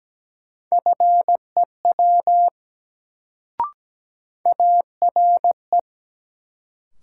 Morse code, heard as a keyed tone, about 17 wpm, 700 Hz. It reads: FEW ARE
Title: Sets of 2 Words - New Word Drill - 17wpm